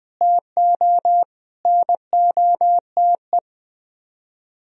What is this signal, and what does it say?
Morse code, heard as a keyed tone, about 20 wpm, 700 Hz. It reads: TO NOTE